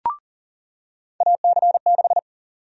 Morse code, heard as a keyed tone, about 40 wpm, 700 Hz. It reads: AC6
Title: USA Call Sign Prefixes - 40wpm